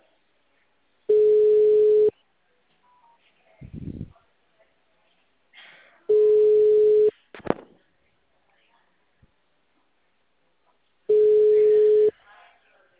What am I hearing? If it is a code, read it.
Ringback tone (Europe, 425 Hz), with speech.
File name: out- -2008- - - .15084